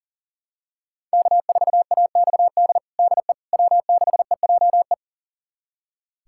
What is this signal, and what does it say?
Morse code, heard as a keyed tone, about 40 wpm, 700 Hz. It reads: K4AXD DE W6EJE
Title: USA Callsign DE Callsign 40wpm